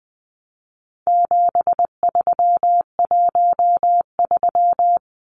Morse code, heard as a keyed tone, about 19 wpm, 700 Hz. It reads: 7313
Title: Numbers - 4-Digit 20wpm